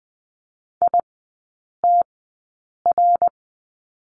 Morse code, heard as a keyed tone, about 20 wpm, spaced as for 8 wpm, 700 Hz. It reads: ITR
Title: 053 Lesson 10e - Letter r - Three Characters 20wpm